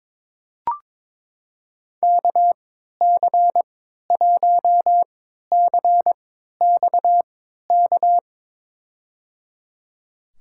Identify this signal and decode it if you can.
Morse code, heard as a keyed tone, about 22 wpm, spaced as for 12 wpm, 700 Hz. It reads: KC1CXK
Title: USA Call Signs - 6 Characters - 3x Character Spacing - 22wpm